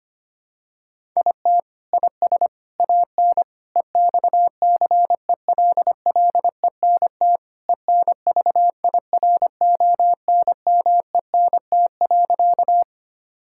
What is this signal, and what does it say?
Morse code, heard as a keyed tone, about 25 wpm, 700 Hz. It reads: IT IS AN EXCELLENT ENVIRONMENT.
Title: Sentences from Top 1000 Words - Review Part 5 - 25wpm